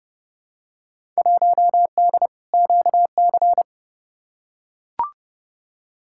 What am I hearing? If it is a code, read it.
Morse code, heard as a keyed tone, about 30 wpm, 700 Hz. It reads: 1D QC
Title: Field Day Exchanges - 30wpm